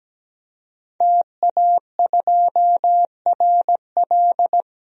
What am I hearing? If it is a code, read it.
Morse code, heard as a keyed tone, about 17 wpm, 700 Hz. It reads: TA2RL
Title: DXCC-100 Call Signs - 5 Characters - 17wpm